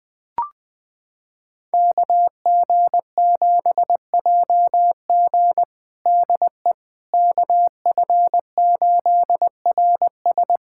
Morse code, heard as a keyed tone, about 21 wpm, 700 Hz. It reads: KG7JG DE KF8RS
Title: USA Callsign DE Callsign - 5 Characters - 20wpm